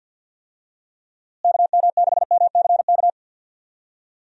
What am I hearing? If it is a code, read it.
Morse code, heard as a keyed tone, about 48 wpm, 700 Hz. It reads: KM6MCK
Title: USA Call Signs - 6 Characters - Repeated 2x - 50wpm